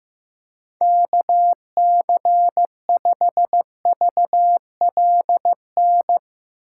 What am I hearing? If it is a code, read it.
Morse code, heard as a keyed tone, about 15 wpm, 700 Hz. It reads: KC5VLN
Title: USA Call Signs - 2x Character Spacing - 15wpm